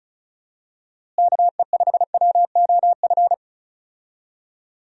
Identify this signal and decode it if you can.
Morse code, heard as a keyed tone, about 34 wpm, 700 Hz. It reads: KE5WOF
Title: USA Call Signs - 6 Characters - Repeated 3x - 35wpm